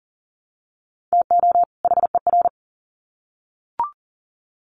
Morse code, heard as a keyed tone, about 40 wpm, 700 Hz. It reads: TO HER
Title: Sets of 2 Words 40wpm